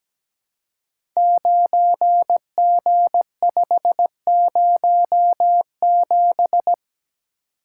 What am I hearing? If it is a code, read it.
Morse code, heard as a keyed tone, about 17 wpm, 700 Hz. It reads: 9G507
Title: Code Groups - 5 Letter:Numbers - 17wpm